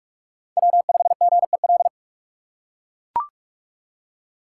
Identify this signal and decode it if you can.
Morse code, heard as a keyed tone, about 42 wpm, 700 Hz. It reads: W5GEL